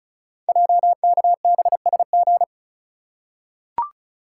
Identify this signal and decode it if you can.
Morse code, heard as a keyed tone, about 36 wpm, 690 Hz. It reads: JKBSG
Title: Code Groups - 5 Letters - 35wpm